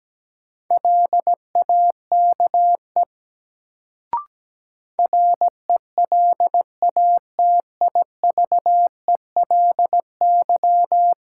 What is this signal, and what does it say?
Morse code, heard as a keyed tone, about 17 wpm, 700 Hz. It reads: LAKE RELATIVELY
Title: Sentences from Top 1100 Words - New Word Drill Alt - 17wpm